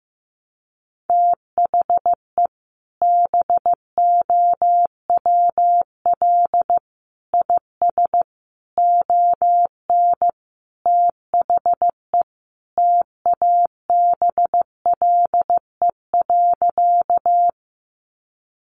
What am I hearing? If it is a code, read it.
Morse code, heard as a keyed tone, about 15 wpm, 700 Hz. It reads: THE BOWL IS ON THE TABLE.
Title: Sentences from Top 1200 Words 15wpm